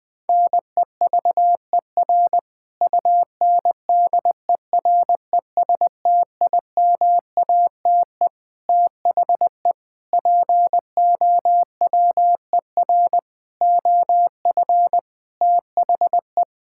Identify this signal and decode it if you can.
Morse code, heard as a keyed tone, about 21 wpm, 700 Hz. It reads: NEVER UNDERESTIMATE THE POWER OF THE
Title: Sentences with 4-Word Phrases - 20wpm